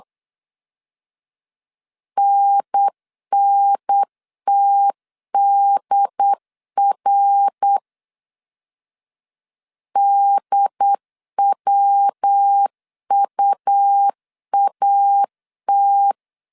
Morse code, heard as a keyed tone, about 8 wpm, 800 Hz. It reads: NNTDR DWUAT
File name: CW / DET / 1 3 1500C S B 8wpm 4141